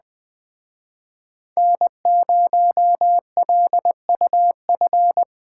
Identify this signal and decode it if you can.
Morse code, heard as a keyed tone, about 20 wpm, 700 Hz. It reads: N0LUF
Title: Speed Racing - USA 1x3 Call Signs 17wpm